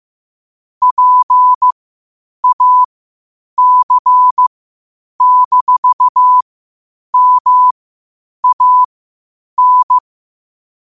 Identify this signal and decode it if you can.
Morse code, heard as a keyed tone, about 15 wpm, spaced as for 8 wpm, 1000 Hz. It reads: PAC-MAN
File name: qotdfiles / 8wpm